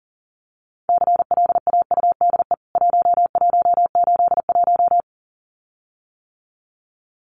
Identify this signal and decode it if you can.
Morse code, heard as a keyed tone, about 40 wpm, 700 Hz. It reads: CLAUDE 1181
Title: CWT Exchange 40wpm